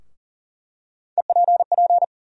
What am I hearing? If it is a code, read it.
Morse code, heard as a keyed tone, about 37 wpm, 700 Hz. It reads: EPP
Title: USA Call Sign Postfixes - 3 Letters 40wpm